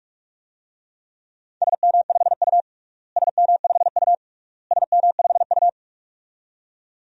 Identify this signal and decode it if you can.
Morse code, heard as a keyed tone, about 46 wpm, 700 Hz. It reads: SM5U SM5U SM5U
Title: DXCC-100 Call Signs - 4 Characters - Repeated 3x - 45wpm